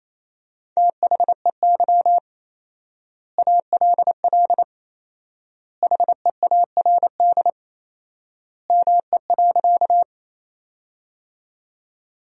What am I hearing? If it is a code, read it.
Morse code, heard as a keyed tone, about 28 wpm, 700 Hz. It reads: THEY ALL HEARD ME.
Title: 4x Word-Spacing - Sentences from Top 400 Words - 28wpm